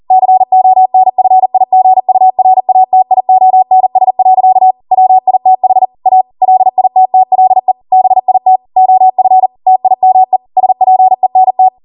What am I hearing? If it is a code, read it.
Morse code, heard as a keyed tone, about 40 wpm, 750 Hz. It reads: CONFIGURATIONS. WITH A LITTLE BIT OF TIME SPENT